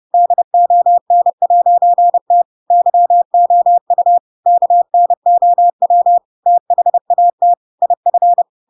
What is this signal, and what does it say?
Morse code, heard as a keyed tone, about 30 wpm, 700 Hz. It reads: DON'T YOU KNOW THAT IF
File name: Pinocchio / Pinocchio ch4 30 30 ik1hge